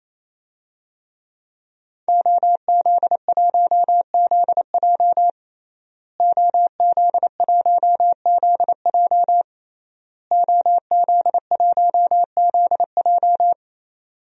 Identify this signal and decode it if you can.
Morse code, heard as a keyed tone, about 27 wpm, 700 Hz. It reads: OZ1ZJ OZ1ZJ OZ1ZJ
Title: DXCC-100 Call Signs - 5 Characters - Repeated 3x - 28wpm